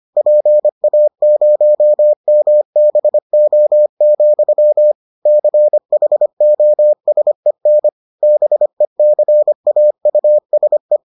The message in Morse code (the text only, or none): PA0MBO, CHOSEN BECAUSE